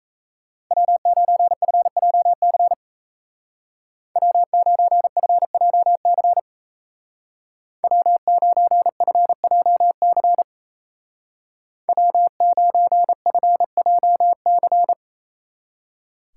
Morse code, heard as a keyed tone, about 33 wpm, 700 Hz. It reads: W9FJC W9FJC W9FJC W9FJC
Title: Speed Racing - USA 1x3 Call Signs 28wpm